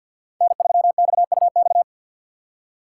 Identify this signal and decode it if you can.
Morse code, heard as a keyed tone, about 59 wpm, 700 Hz. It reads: N3XUX
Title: Speed Racing - USA 1x3 Call Signs 50wpm